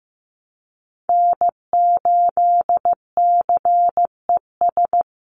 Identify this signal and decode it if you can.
Morse code, heard as a keyed tone, about 15 wpm, 700 Hz. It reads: N8CES
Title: USA Call Signs 15wpm